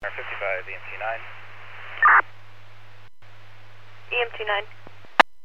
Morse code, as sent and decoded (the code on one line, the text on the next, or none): none